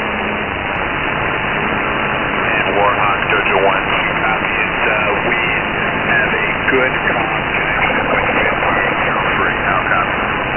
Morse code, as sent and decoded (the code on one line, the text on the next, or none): none